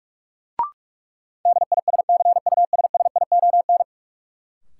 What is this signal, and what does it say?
Morse code, heard as a keyed tone, about 44 wpm, 700 Hz. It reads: DISCUSSION
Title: Sentences from Top 1100 Words - New Word Drill Alt - 45wpm